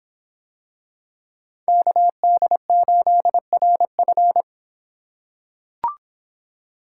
Morse code, heard as a keyed tone, about 26 wpm, 700 Hz. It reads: KD8RF